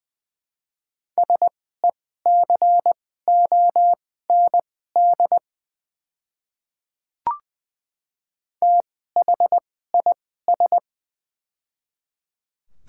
Morse code, heard as a keyed tone, about 21 wpm, spaced as for 15 wpm, 700 Hz. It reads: SECOND THIS
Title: 2x Character Spacing - Top 300 Words - 20wpm